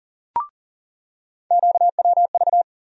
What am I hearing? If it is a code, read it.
Morse code, heard as a keyed tone, about 40 wpm, 700 Hz. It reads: QWV
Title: USA Call Sign Postfixes - 3 Letters 40wpm